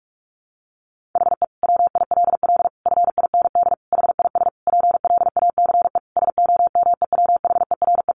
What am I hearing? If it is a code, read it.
Morse code, heard as a keyed tone, about 44 wpm, 700 Hz. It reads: HE WILL FIND HIS PLACE SOMEWHERE
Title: Sentences from Top 1400 Words 45wpm